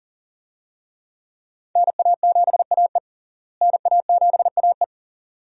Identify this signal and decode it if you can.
Morse code, heard as a keyed tone, about 40 wpm, 700 Hz. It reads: NA7AE NA7AE
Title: USA Call Signs - 5 Characters - Repeated 2x - 40wpm